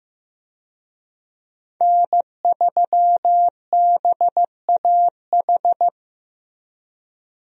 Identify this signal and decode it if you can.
Morse code, heard as a keyed tone, about 15 wpm, 700 Hz. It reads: N3BAH
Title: USA Call Signs - 5 Characters - Repeated 2x - 15wpm